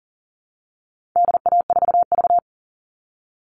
Morse code, heard as a keyed tone, about 37 wpm, 700 Hz. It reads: DA4V
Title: All Cut Numbers - 4-Digit 40wpm